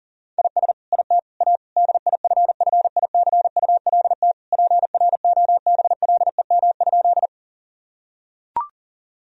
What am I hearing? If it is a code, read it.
Morse code, heard as a keyed tone, about 40 wpm, 700 Hz. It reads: IS IT A DIFFICULT PROBLEM?
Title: Sentences from Top 1000 Words - Review Part 2 - 40wpm